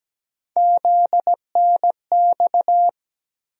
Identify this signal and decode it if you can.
Morse code, heard as a keyed tone, about 17 wpm, 700 Hz. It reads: ZNX
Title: USA Call Sign Postfixes - 3 Letters 17wpm